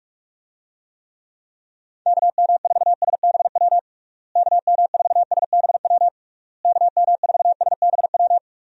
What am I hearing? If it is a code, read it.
Morse code, heard as a keyed tone, about 46 wpm, 700 Hz. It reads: KM4SBW KM4SBW KM4SBW
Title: USA Call Signs - 6 Characters - Repeated 3x - 45wpm